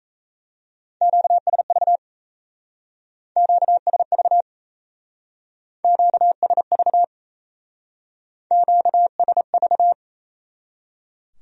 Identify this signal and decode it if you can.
Morse code, heard as a keyed tone, about 36 wpm, 700 Hz. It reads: QSV QSV QSV QSV